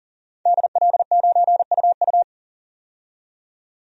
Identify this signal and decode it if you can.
Morse code, heard as a keyed tone, about 38 wpm, 700 Hz. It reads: DL9UU
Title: DXCC-100 Call Signs - 5 Characters - 40wpm